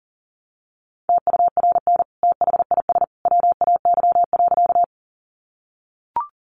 Morse code, heard as a keyed tone, about 40 wpm, 700 Hz. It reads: TURN THIS WAY.
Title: Sentences from Top 200 Words 40wpm